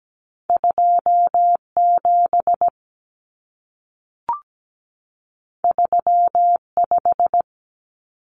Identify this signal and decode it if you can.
Morse code, heard as a keyed tone, about 17 wpm, 700 Hz. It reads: 27 35
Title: Common Cut Numbers - 2-Digit 17wpm